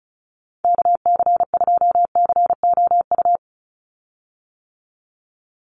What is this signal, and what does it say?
Morse code, heard as a keyed tone, about 34 wpm, 700 Hz. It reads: KC2COU
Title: USA Call Signs 35wpm